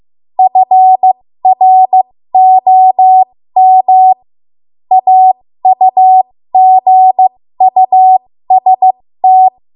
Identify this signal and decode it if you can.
Morse code, heard as a keyed tone, about 15 wpm, spaced as for 13 wpm, 750 Hz. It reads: FROM AUGUST
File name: Morse / 13WPM